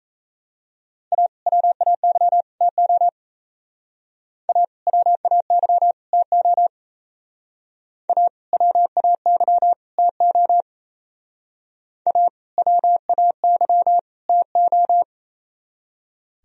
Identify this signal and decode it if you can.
Morse code, heard as a keyed tone, about 35 wpm, 700 Hz. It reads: A WAY TO A WAY TO A WAY TO A WAY TO